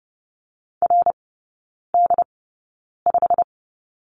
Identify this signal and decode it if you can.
Morse code, heard as a keyed tone, about 30 wpm, spaced as for 9 wpm, 700 Hz. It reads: RD5
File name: course / 30:10wpm / 087 Lesson 15e - Number 5 - Three Characters 30wpm